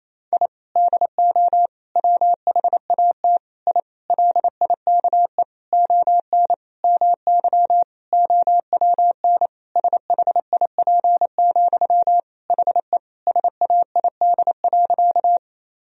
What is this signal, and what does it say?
Morse code, heard as a keyed tone, about 27 wpm, 700 Hz. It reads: I DO WHAT I LIKE ON MY OWN SHIP, HE SAID.
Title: Sentences from Top 1000 Words - Review Part 2 - 28wpm